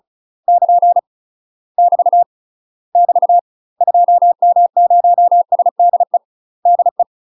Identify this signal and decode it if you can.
Morse code, heard as a keyed tone, about 34 wpm, 700 Hz. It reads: <KN> <BT> <BT> 2M0SDE DE
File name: UK MASTER QSOs TRAINING / 35WPM / UK MASTER QSOs 0001